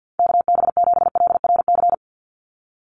Morse code, heard as a keyed tone, about 50 wpm, 700 Hz. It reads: KB7LRC
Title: USA Call Signs 50wpm